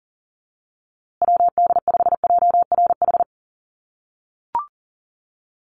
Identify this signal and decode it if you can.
Morse code, heard as a keyed tone, about 40 wpm, 700 Hz. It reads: WD5JRH
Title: USA Call Signs Repeated 2x 40wpm